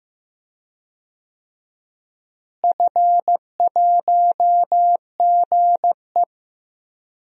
Morse code, heard as a keyed tone, about 15 wpm, 700 Hz. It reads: F1GE
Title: DXCC-100 Call Signs - 4 Characters - Repeated 2x - 15wpm